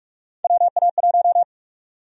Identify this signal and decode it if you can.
Morse code, heard as a keyed tone, about 42 wpm, 700 Hz. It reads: WA1
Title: USA Call Sign Prefixes - Repeated 2x - 45wpm